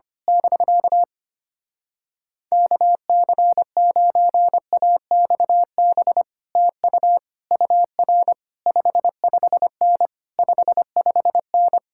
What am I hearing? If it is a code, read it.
Morse code, heard as a keyed tone, about 24 wpm, 700 Hz. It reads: <BK> KC9AXB TU UR 55N 55N
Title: Speed-Racing Common QSO Phrases - 25wpm